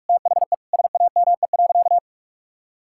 Morse code, heard as a keyed tone, about 46 wpm, 700 Hz. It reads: THE SAME.
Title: Sentences from Top 1000 Words - Review Part 6 - 45wpm